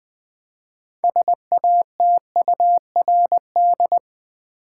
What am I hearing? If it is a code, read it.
Morse code, heard as a keyed tone, about 20 wpm, 700 Hz. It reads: SATURD
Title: Build A Word - 2 Letters At A Time - 8-Letter Words - 20wpm